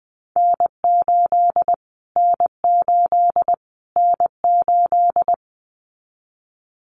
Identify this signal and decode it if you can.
Morse code, heard as a keyed tone, about 20 wpm, 700 Hz. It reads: N8 N8 N8
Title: Callsign Prefixes for Top 25 DXCC - Repeated 3x - 20wpm